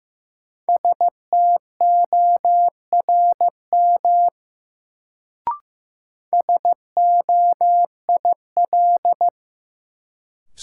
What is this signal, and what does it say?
Morse code, heard as a keyed tone, about 15 wpm, 700 Hz. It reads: STORM SOIL
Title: Sentences from Top 1500 Words - New Word Drill Alt - 15wpm